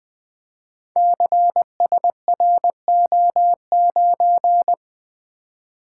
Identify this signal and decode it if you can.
Morse code, heard as a keyed tone, about 20 wpm, 700 Hz. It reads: CSRO9
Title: Code Groups - 5 Letter:Numbers - 20wpm